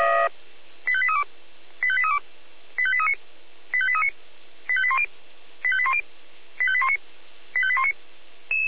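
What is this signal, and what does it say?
Five-tone selective call (ZVEI), heard as quick runs of five tones, tones of about 70 ms. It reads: 85832 85832 85839 85839 85810 85810 85819 85819
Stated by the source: Received by SDR.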